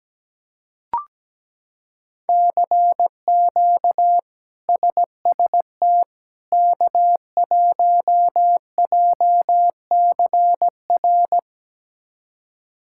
Morse code, heard as a keyed tone, about 17 wpm, 700 Hz. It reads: CQ SST K1JCR
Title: K1USN SST QSO Practice - 17wpm